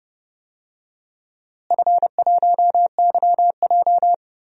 Morse code, heard as a keyed tone, about 30 wpm, 700 Hz. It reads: F1YJ
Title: DXCC-100 Call Signs - 4 Characters - Repeated 2x - 30wpm